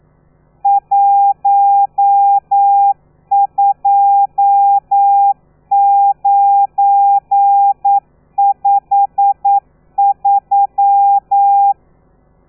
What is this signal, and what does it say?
Morse code, heard as a keyed tone, about 9 wpm, 800 Hz. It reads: 12953
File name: telegrafia / i2tfj / Lez12 40